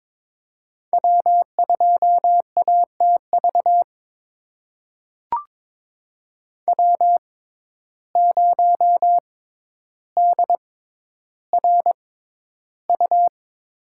Morse code, heard as a keyed tone, about 21 wpm, spaced as for 11 wpm, 700 Hz. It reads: W2ATV W0DRU
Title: USA Call Signs - 5 Characters - 6x Character Spacing - 22wpm